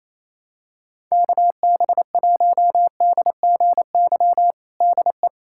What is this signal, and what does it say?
Morse code, heard as a keyed tone, about 27 wpm, 700 Hz. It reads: KB1DGY DE
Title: USA Callsign DE Callsign - 6 Characters - 28wpm